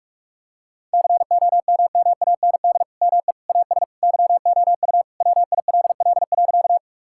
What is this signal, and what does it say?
Morse code, heard as a keyed tone, about 46 wpm, 700 Hz. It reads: COMMAND ME AS YOU WILL.
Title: Sentences from Top 1600 Words 45wpm